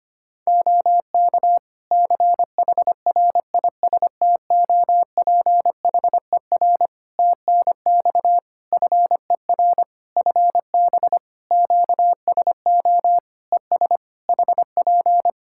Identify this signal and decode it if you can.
Morse code, heard as a keyed tone, about 25 wpm, 700 Hz. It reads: OK CHRISTOPHER TNX FER FB QSO ES HP
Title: Practice QSO - Exchange 4 - Ending - 25wpm